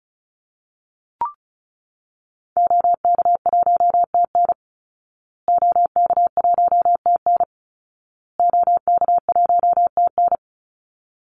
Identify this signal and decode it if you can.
Morse code, heard as a keyed tone, about 34 wpm, 700 Hz. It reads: OK1TN OK1TN OK1TN
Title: Call Signs Repeated 2x - 35wpm